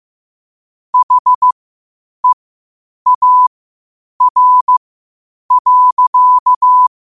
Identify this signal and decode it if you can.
Morse code, heard as a keyed tone, about 15 wpm, spaced as for 8 wpm, 1000 Hz. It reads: HEAR.
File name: qotdfiles / 8wpm